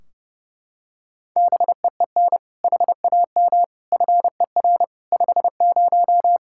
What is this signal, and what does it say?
Morse code, heard as a keyed tone, about 30 wpm, 700 Hz. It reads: BEEN HAM FER 50
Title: Practice QSO - Exchange 3 - More Info - 30wpm